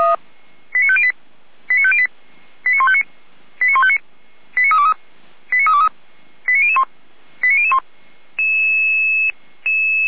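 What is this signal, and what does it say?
Five-tone selective call (ZVEI), heard as quick runs of five tones, tones of about 70 ms. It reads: 89508 89508 89159 89159 89323 89323 89001 89001
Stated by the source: ZVEI five-tone call received by SDR, fire brigade Feuerwehr Orlamünde.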